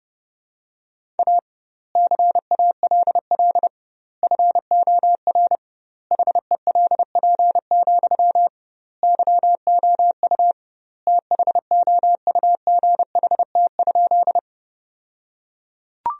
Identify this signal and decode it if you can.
Morse code, heard as a keyed tone, about 30 wpm, 700 Hz. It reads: A CALL FOR HELP, YOU THOUGHT?
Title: 2x Word-Spacing - Sentences from Top 300 Words - 30wpm